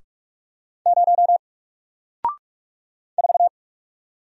Morse code, heard as a keyed tone, about 46 wpm, 700 Hz. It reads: 0 4
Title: Single Letter-Number Alt 45wpm